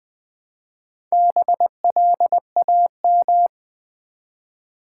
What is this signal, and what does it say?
Morse code, heard as a keyed tone, about 20 wpm, 700 Hz. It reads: BLAM
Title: Build A Word - 1 Letter At A Time - 3 to 5-Letter Words - 20wpm